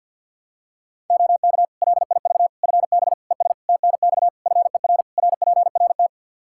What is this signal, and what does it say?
Morse code, heard as a keyed tone, about 51 wpm, 700 Hz. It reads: OK LIV FB ES TNX FER RPRT
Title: Practice QSO - Exchange 2 - Info - 50wpm